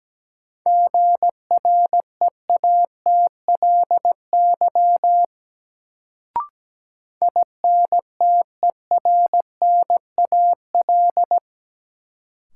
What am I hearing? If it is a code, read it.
Morse code, heard as a keyed tone, about 17 wpm, 700 Hz. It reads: GREATLY INTERNAL
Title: Sentences from Top 1400 Words - New Word Drill Alt - 17wpm